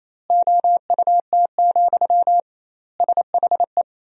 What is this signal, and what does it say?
Morse code, heard as a keyed tone, about 28 wpm, 700 Hz. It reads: OUT, SHE
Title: 2x Word-Spacing - Sentences from Top 500 Words - 28wpm